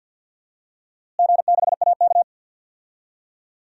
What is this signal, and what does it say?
Morse code, heard as a keyed tone, about 51 wpm, 700 Hz. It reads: G6AK
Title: DXCC-100 Call Signs - 4 Characters - Repeated 2x - 50wpm